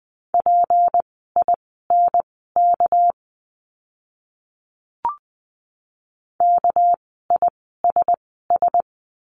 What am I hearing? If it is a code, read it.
Morse code, heard as a keyed tone, about 20 wpm, spaced as for 14 wpm, 700 Hz. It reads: PINK KISS
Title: 199 Lesson 32f - Letter k - Words 20wpm